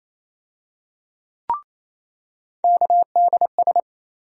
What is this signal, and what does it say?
Morse code, heard as a keyed tone, about 28 wpm, 700 Hz. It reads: KDS
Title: USA Call Sign Postfixes - 3 Letters 28wpm